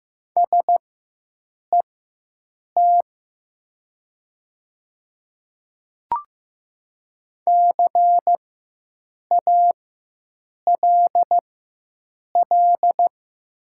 Morse code, heard as a keyed tone, about 15 wpm, spaced as for 7 wpm, 700 Hz. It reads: SET CALL
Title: 4x Character Spacing - Top 200 Words - 15wpm